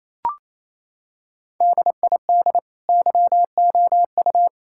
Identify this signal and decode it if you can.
Morse code, heard as a keyed tone, about 28 wpm, 700 Hz. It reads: DID YOU
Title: Sentences from Top 1600 Words 28wpm